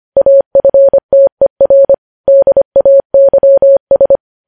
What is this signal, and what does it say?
Morse code, heard as a keyed tone, about 25 wpm, 550 Hz. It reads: AFTER DAYS
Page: news-headlines-in-morse-code-at-25-wpm